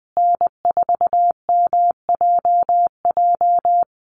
Morse code, heard as a keyed tone, about 20 wpm, 700 Hz. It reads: N4MJJ
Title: USA Call Signs Repeated 3x 20wpm